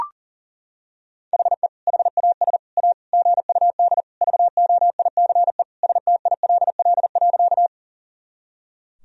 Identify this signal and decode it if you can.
Morse code, heard as a keyed tone, about 40 wpm, 700 Hz. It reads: HE HAS A GUD VOICE STILL.
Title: Sentences from Top 500 Words and Abbreviations 40wpm